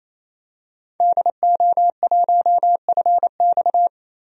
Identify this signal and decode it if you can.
Morse code, heard as a keyed tone, about 28 wpm, 700 Hz. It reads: DO1FX
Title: DXCC-100 Call Signs - 5 Characters - Repeated 3x - 28wpm